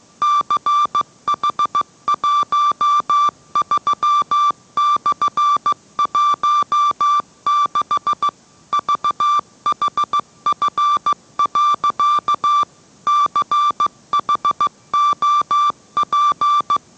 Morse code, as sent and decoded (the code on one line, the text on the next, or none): -.-. .... .---- ...-- -..-. .---- -.... / ...- .... ..-. .-.-.- / -.-. .... --- .--.
CH13/16 VHF. CHOP